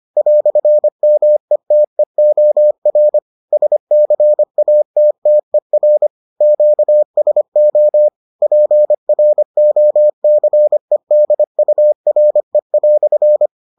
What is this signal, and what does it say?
Morse code, heard as a keyed tone, about 25 wpm, 600 Hz. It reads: "METEOR SCATTER QSO PROCEDURE"